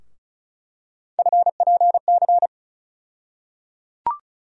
Morse code, heard as a keyed tone, about 36 wpm, 700 Hz. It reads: FPC
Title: USA Call Sign Postfixes - 3 Letters 35wpm